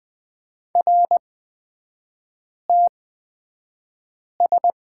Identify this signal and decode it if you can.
Morse code, heard as a keyed tone, about 20 wpm, spaced as for 5 wpm, 700 Hz. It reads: RTS
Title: 243 Lesson 39e - Letter z - Three Characters 20wpm